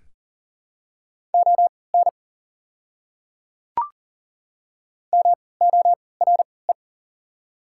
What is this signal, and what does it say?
Morse code, heard as a keyed tone, about 38 wpm, spaced as for 22 wpm, 690 Hz. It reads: ON MORE
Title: 3x Character Spacing - Top 100 Words - 40wpm